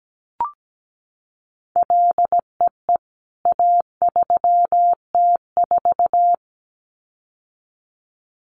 Morse code, heard as a keyed tone, about 17 wpm, 700 Hz. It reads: LEE A3T4
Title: CWT Exchange 17wpm